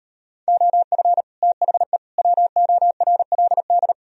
Morse code, heard as a keyed tone, about 38 wpm, 700 Hz. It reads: OF THE WORLD